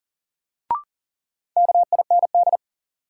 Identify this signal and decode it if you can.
Morse code, heard as a keyed tone, about 40 wpm, 700 Hz. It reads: KIND
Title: Sets of 2 Words - New Word Drill - 40wpm